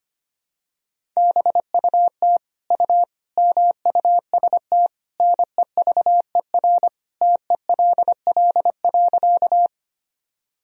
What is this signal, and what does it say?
Morse code, heard as a keyed tone, about 25 wpm, 700 Hz. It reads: BUT U MUST NEVER TELL.
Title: Sentences from Top 300 Words and Abbreviations 25wpm